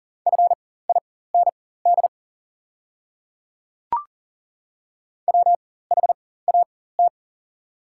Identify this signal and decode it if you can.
Morse code, heard as a keyed tone, about 40 wpm, spaced as for 19 wpm, 700 Hz. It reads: FIND WHAT
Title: 4x Character Spacing - Top 200 Words - 40wpm